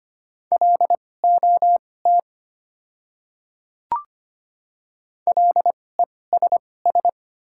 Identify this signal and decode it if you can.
Morse code, heard as a keyed tone, about 25 wpm, spaced as for 18 wpm, 700 Hz. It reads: LOT LESS
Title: 2x Character Spacing - Top 500 Words - 25wpm